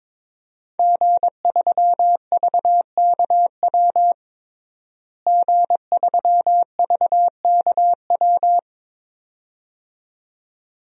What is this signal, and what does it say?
Morse code, heard as a keyed tone, about 22 wpm, 700 Hz. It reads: G3VKW G3VKW
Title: DXCC-100 Call Signs - 5 Characters - Repeated 2x - 22wpm